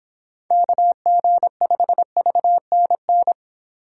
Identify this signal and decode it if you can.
Morse code, heard as a keyed tone, about 29 wpm, 700 Hz. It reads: KG5VNN